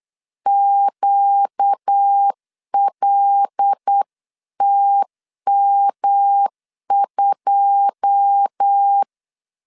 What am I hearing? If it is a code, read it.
Morse code, heard as a keyed tone, about 8 wpm, 800 Hz. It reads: QLTM2
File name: LEZIONI / 9 2 8wpm 3437